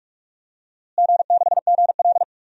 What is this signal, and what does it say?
Morse code, heard as a keyed tone, about 46 wpm, 700 Hz. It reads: G6GL
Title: DXCC-100 Call Signs - 4 Characters - Repeated 2x - 45wpm